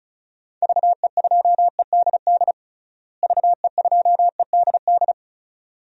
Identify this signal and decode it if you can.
Morse code, heard as a keyed tone, about 36 wpm, 700 Hz. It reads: VE2EDD VE2EDD
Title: DXCC-100 Call Signs - 6 Characters - Repeated 2x - 35wpm